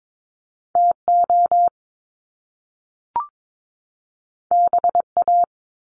Morse code, heard as a keyed tone, about 22 wpm, 700 Hz. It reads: TO BA
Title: Top Two Letter Combinations in Words 22wpm